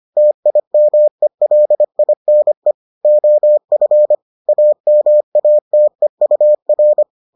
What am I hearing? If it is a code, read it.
Morse code, heard as a keyed tone, about 25 wpm, 600 Hz. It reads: TIMELINE OF AMATEUR